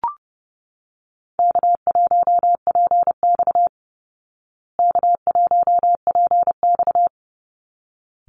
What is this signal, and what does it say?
Morse code, heard as a keyed tone, about 30 wpm, 700 Hz. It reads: K1PX K1PX
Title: Easy Call Signs Repeated 2x - 30wpm